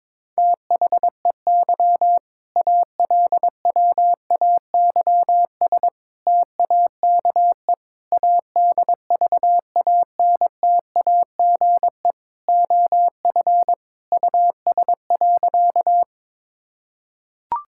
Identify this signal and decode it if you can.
Morse code, heard as a keyed tone, about 22 wpm, 700 Hz. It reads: THEY ALWAYS TAKE ADVANTAGE OF US.
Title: Sentences from Top 1000 Words - Review Part 6 - 22wpm